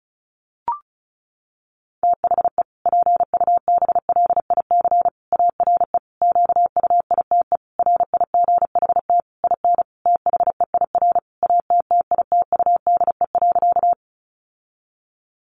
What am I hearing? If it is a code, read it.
Morse code, heard as a keyed tone, about 34 wpm, 700 Hz. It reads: THE PUBLIC ARE QUITE RIGHT IN THEIR ATTITUDE.
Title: Sentences from Top 1300 Words 35wpm